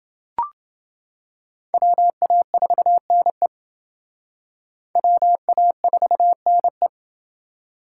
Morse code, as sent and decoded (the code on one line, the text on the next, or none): .-- .- ....- -. / .-- .- ....- -. .
WA4NE WA4NE